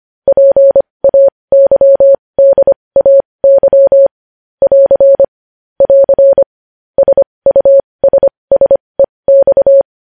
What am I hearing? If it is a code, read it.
Morse code, heard as a keyed tone, about 25 wpm, spaced as for 20 wpm, 550 Hz. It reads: PAYDAY <AR> <AR> SUSSEX
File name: podcast / News headlines-1x-25x20